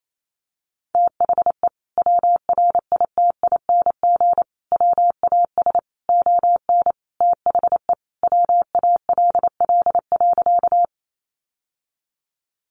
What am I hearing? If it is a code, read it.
Morse code, heard as a keyed tone, about 28 wpm, 700 Hz. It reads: THE WRITING WAS ON THE WALL.
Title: Sentences from Top 1100 Words 28wpm